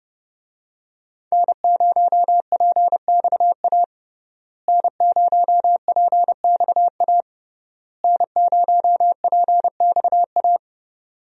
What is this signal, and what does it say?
Morse code, heard as a keyed tone, about 31 wpm, 700 Hz. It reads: N0PXA N0PXA N0PXA
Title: USA Call Signs - 5 Characters - Repeated 3x - 30wpm